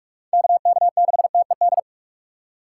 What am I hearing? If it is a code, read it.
Morse code, heard as a keyed tone, about 41 wpm, 700 Hz. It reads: K K 6 T E D
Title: USA Call Signs - 6 Characters - 6x Character Spacing - 45wpm